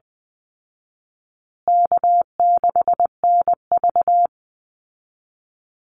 Morse code, heard as a keyed tone, about 20 wpm, 700 Hz. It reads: K6NV
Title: Easy Call Signs Repeated 3x - 20wpm